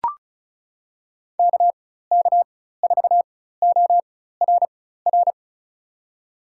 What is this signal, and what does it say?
Morse code, heard as a keyed tone, about 36 wpm, spaced as for 16 wpm, 690 Hz. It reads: KK4ORR
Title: USA Call Signs - 6 Characters - 4x Character Spacing - 35wpm